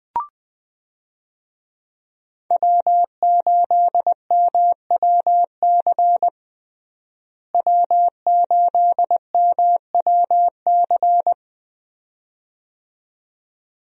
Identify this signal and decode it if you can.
Morse code, heard as a keyed tone, about 20 wpm, 700 Hz. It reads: W8MWC W8MWC
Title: USA Call Signs - 5 Characters - Repeated 2x - 20wpm